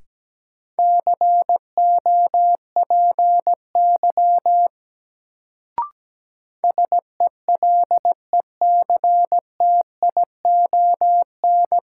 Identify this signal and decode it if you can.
Morse code, heard as a keyed tone, about 17 wpm, 700 Hz. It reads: COPY SELECTION
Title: Sentences from Top 1400 Words - New Word Drill Alt - 17wpm